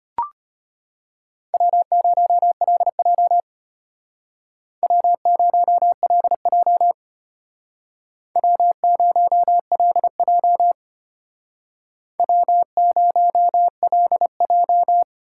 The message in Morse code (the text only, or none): W0LJ W0LJ W0LJ W0LJ